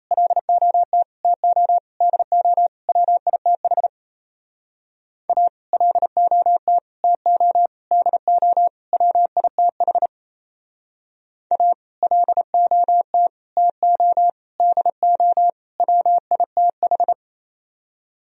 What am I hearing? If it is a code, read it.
Morse code, heard as a keyed tone, about 38 wpm, 700 Hz. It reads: LOT TO DO WITH A LOT TO DO WITH A LOT TO DO WITH